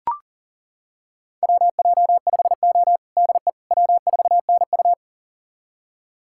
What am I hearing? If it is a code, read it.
Morse code, heard as a keyed tone, about 37 wpm, 700 Hz. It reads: WJ5O DE W4NU
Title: USA Callsign DE Callsign - 4 Characters - 40wpm